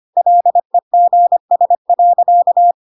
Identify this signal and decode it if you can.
Morse code, heard as a keyed tone, about 25 wpm, 700 Hz. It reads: LEGS.
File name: Pinocchio / Pinocchio ch2 25 25 ik1hge